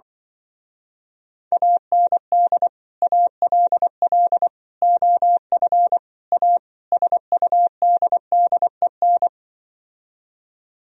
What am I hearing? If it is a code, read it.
Morse code, heard as a keyed tone, about 24 wpm, 700 Hz. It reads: AND ALL OF A SUDDEN
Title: Speed-Racing - Sets of 5 Words 20wpm